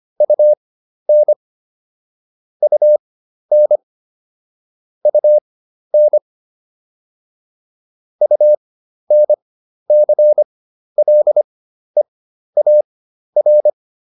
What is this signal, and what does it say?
Morse code, heard as a keyed tone, about 25 wpm, spaced as for 12 wpm, 600 Hz. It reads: UN UN UN UNCLEAR